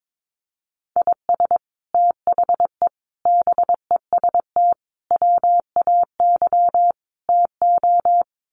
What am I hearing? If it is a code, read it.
Morse code, heard as a keyed tone, about 22 wpm, 700 Hz. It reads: IS THE BEST WAY TO
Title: Sets of 5 Words Encore 22wpm